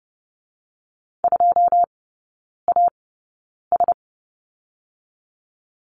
Morse code, heard as a keyed tone, about 30 wpm, 700 Hz. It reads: 2 A S
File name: 081 Lesson 14e - Number 2 - Three Characters 30wpm